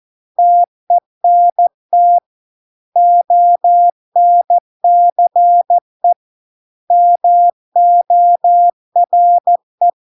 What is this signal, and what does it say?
Morse code, heard as a keyed tone, about 14 wpm, 700 Hz. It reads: TENT ONCE MORE